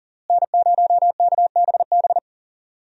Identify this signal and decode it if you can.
Morse code, heard as a keyed tone, about 40 wpm, 700 Hz. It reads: N0KBB
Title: USA Call Signs - 5 Characters - Repeated 2x - 40wpm